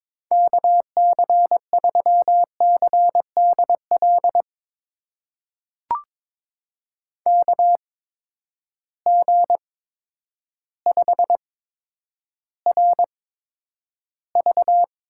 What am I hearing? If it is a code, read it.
Morse code, heard as a keyed tone, about 22 wpm, spaced as for 10 wpm, 700 Hz. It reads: KC3CDL KG5RV
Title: USA Call Signs - 6 Characters - 8x Character Spacing - 22wpm